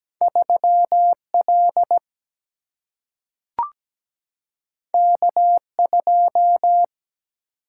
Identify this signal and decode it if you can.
Morse code, heard as a keyed tone, about 17 wpm, 700 Hz. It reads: 3L K2
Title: Single-Letter Single-Number 17wpm